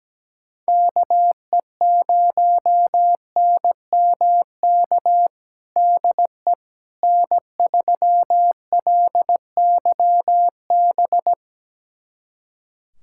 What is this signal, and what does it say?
Morse code, heard as a keyed tone, about 17 wpm, 700 Hz. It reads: KE0NMK DE N3LYB
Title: USA Callsign DE Callsign 17wpm